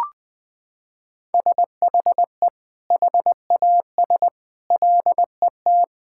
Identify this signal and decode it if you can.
Morse code, heard as a keyed tone, about 20 wpm, 700 Hz. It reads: SHE HAS LET
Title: Sentences from Top 1000 Words - Review Part 6 - 20wpm